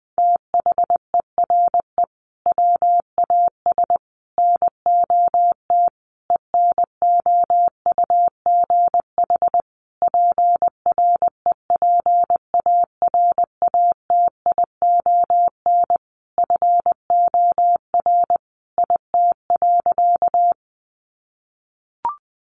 Morse code, heard as a keyed tone, about 20 wpm, 700 Hz. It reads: THERE WAS NOT ENOUGH PREPARATION FOR IT.
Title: Sentences from Top 1400 Words 20wpm